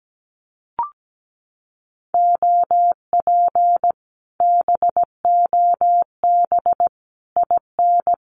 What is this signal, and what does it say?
Morse code, heard as a keyed tone, about 17 wpm, 700 Hz. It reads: OP BOB IN
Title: Common QSO Phrases Encore 17wpm